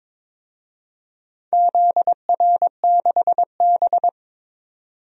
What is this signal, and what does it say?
Morse code, heard as a keyed tone, about 23 wpm, 700 Hz. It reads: ZR6B
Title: DXCC-100 Call Signs - 4 Characters - 22wpm